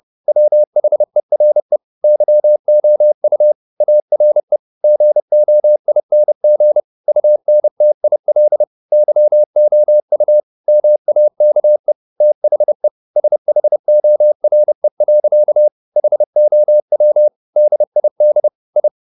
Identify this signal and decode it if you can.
Morse code, heard as a keyed tone, about 30 wpm, 600 Hz. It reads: WHERE YOU ARE GOING UNTIL YOU MAKE THE SHORE. HOW DID I
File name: the old man and the sea / the old man and the sea 1 30wpm